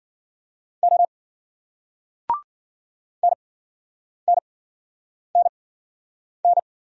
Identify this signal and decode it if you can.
Morse code, heard as a keyed tone, about 50 wpm, spaced as for 9 wpm, 700 Hz. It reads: O NNNN